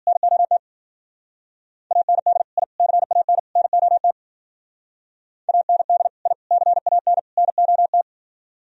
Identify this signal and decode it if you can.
Morse code, heard as a keyed tone, about 47 wpm, 700 Hz. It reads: NOT AND I CAN NOT AND I CAN NOT
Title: Speed-Racing - Sets of 4 Words 40wpm